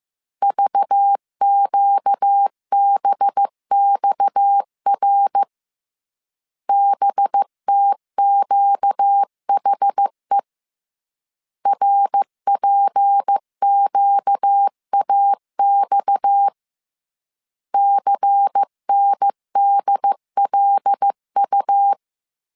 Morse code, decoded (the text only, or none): VQBXR BTQHE RPQAX CNDLU